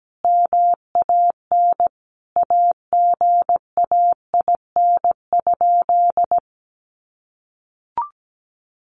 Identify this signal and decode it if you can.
Morse code, heard as a keyed tone, about 17 wpm, 700 Hz. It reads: MAN AGAIN?
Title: Sentences from Top 1400 Words 17wpm